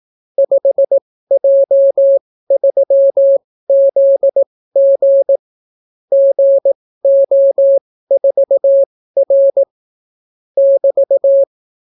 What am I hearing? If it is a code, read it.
Morse code, heard as a keyed tone, about 18 wpm, spaced as for 14 wpm, 550 Hz. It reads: 5J3ZG GO4R <BT>